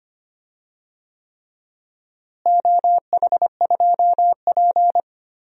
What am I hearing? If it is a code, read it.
Morse code, heard as a keyed tone, about 25 wpm, 700 Hz. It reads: OH2P
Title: DXCC-100 Call Signs - 4 Characters - Repeated 3x - 25wpm